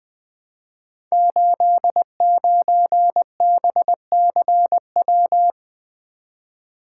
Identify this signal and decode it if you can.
Morse code, heard as a keyed tone, about 20 wpm, 700 Hz. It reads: 89BCW
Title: Code Groups - 5 Letter:Numbers - 20wpm